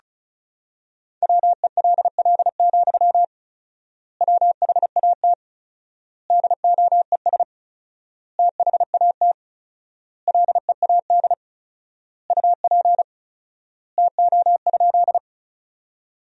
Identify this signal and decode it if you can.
Morse code, heard as a keyed tone, about 34 wpm, 700 Hz. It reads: WELL, WHAT DOES THAT LEAD UP TO?
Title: 4x Word-Spacing - Sentences from Top 500 Words - 35wpm